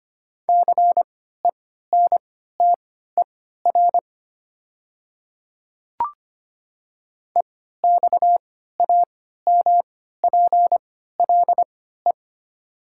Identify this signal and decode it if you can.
Morse code, heard as a keyed tone, about 25 wpm, spaced as for 14 wpm, 700 Hz. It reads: CENTER EXAMPLE
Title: 3x Character Spacing - Top 500 Words - 25wpm